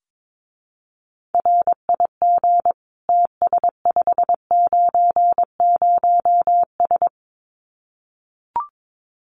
Morse code, heard as a keyed tone, about 22 wpm, 700 Hz. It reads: RIG TS590S
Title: Common QSO Phrases Encore 22wpm